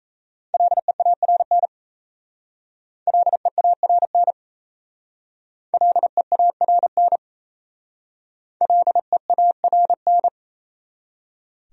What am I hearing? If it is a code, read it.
Morse code, heard as a keyed tone, about 34 wpm, 700 Hz. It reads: LEARN LEARN LEARN LEARN
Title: Speed-Racing Top 300 Words 28wpm